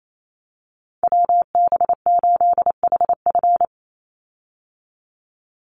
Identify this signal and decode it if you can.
Morse code, heard as a keyed tone, about 28 wpm, 700 Hz. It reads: WB8HF
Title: Call Signs Repeated 3x - 28wpm